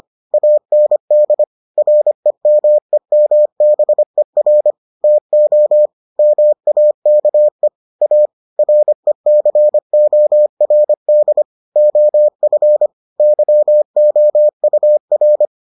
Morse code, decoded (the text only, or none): AND REMEMBER TO MAKE A RECORD OF YOUR